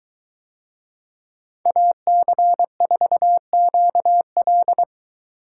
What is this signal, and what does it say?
Morse code, heard as a keyed tone, about 22 wpm, 700 Hz. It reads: AC4QL